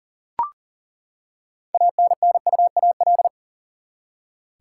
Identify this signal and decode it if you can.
Morse code, heard as a keyed tone, about 40 wpm, 700 Hz. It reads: ANNUAL
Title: Six Letter Words - 40wpm